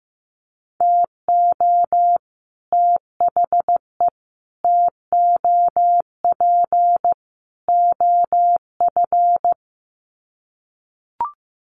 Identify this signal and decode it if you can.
Morse code, heard as a keyed tone, about 15 wpm, 700 Hz. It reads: TO THE TOP OF
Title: Sets of 4 Words 15wpm